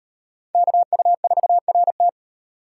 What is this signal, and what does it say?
Morse code, heard as a keyed tone, about 34 wpm, 700 Hz. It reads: KU4RT